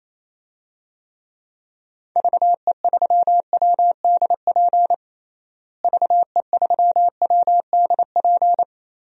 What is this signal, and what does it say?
Morse code, heard as a keyed tone, about 28 wpm, 700 Hz. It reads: VE3WDP VE3WDP
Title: DXCC-100 Call Signs - 6 Characters - Repeated 3x - 28wpm